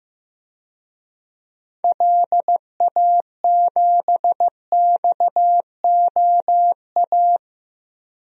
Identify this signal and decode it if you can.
Morse code, heard as a keyed tone, about 15 wpm, 700 Hz. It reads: LA7XOA
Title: DXCC-100 Call Signs - 6 Characters - 15wpm